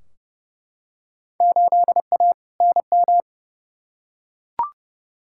Morse code, heard as a keyed tone, about 30 wpm, 700 Hz. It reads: 8A NM
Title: Field Day Exchanges Encore - 30wpm